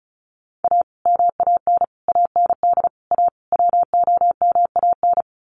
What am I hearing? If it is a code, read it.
Morse code, heard as a keyed tone, about 34 wpm, 700 Hz. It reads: A MAN AND A WOMAN
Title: Sets of 5 Words Encore 35wpm